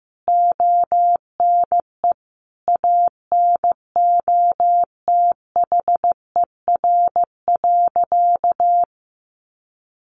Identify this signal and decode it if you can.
Morse code, heard as a keyed tone, about 14 wpm, 700 Hz. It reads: ONE ANOTHER.
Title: Sentences from Top 600 Words 15wpm